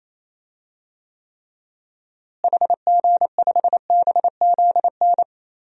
Morse code, heard as a keyed tone, about 28 wpm, 700 Hz. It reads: HG5BZN
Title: DXCC-100 Call Signs - 6 Characters - Repeated 3x - 28wpm